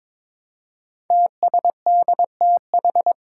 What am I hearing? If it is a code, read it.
Morse code, heard as a keyed tone, about 21 wpm, 700 Hz. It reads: TSDTH
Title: Five Challenging Letter-Number Combinations 22wpm